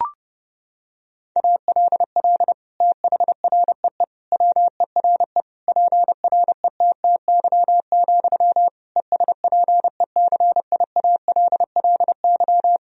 Morse code, heard as a keyed tone, about 30 wpm, 700 Hz. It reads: ALL THREE WERE PRETTY, ESPECIALLY
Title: Sentences from Top 1000 Words - Review Part 4 - 30wpm